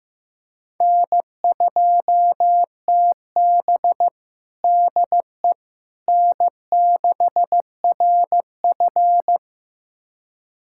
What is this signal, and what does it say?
Morse code, heard as a keyed tone, about 15 wpm, 700 Hz. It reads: N2TB DE N6RF
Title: USA Callsign DE Callsign - 4 Characters - 15wpm